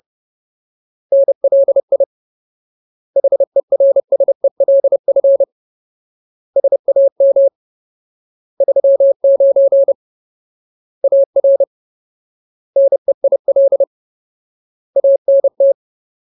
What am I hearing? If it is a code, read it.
Morse code, heard as a keyed tone, about 30 wpm, 550 Hz. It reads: NLI HERSELF SAM 39 AR NEIL ANT
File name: Ultimate Word List / 30WPM Ultimate